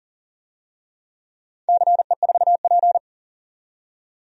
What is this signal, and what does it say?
Morse code, heard as a keyed tone, about 40 wpm, 700 Hz. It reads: CE4P
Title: DXCC-100 Call Signs - 4 Characters - 40wpm